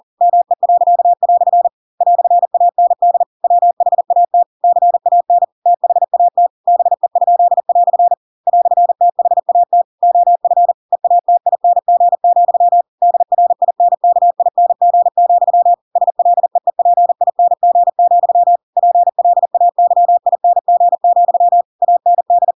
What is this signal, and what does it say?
Morse code, heard as a keyed tone, about 40 wpm, 700 Hz. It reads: ME." "AND WHAT CAN THAT BE?" "THAT OF EATING, DRINKING, SLEEPING, PLAYING, AND
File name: Pinocchio / Pinocchio ch4 40 40 ik1hge